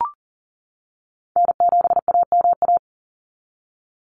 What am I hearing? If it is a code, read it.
Morse code, heard as a keyed tone, about 40 wpm, 700 Hz. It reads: N7AMA
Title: NA SOTA Chaser Callsigns 40wpm